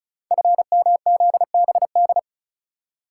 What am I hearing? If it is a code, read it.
Morse code, heard as a keyed tone, about 34 wpm, 700 Hz. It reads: FMZBD
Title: Code Groups - 5 Letters - 35wpm